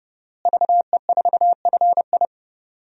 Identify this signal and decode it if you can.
Morse code, heard as a keyed tone, about 32 wpm, 700 Hz. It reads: VE4FI
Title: DXCC-100 Call Signs - 5 Characters - Repeated 3x - 30wpm